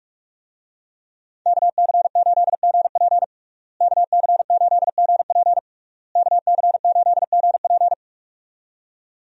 Morse code, heard as a keyed tone, about 46 wpm, 700 Hz. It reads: KC8GP KC8GP KC8GP
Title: USA Call Signs - 5 Characters - Repeated 3x - 45wpm